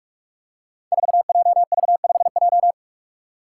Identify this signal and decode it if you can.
Morse code, heard as a keyed tone, about 41 wpm, 700 Hz. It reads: 4JV5J
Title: Five Challenging Letter-Number Combinations 45wpm